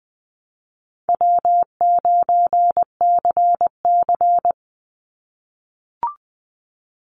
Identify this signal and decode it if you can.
Morse code, heard as a keyed tone, about 20 wpm, 700 Hz. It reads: W9CC
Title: Easy Call Signs - 20wpm